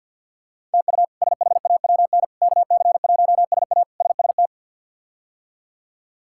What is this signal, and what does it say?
Morse code, heard as a keyed tone, about 47 wpm, 700 Hz. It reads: TU SHAWN KC1SA SST
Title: K1USN SST QSO Practice - 50wpm